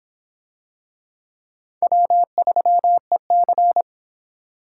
Morse code, heard as a keyed tone, about 22 wpm, 700 Hz. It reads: W3EC